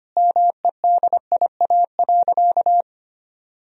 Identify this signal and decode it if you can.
Morse code, heard as a keyed tone, about 25 wpm, 700 Hz. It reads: MEDIA.
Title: Sentences from Top 1000 Words - Review Part 4 - 25wpm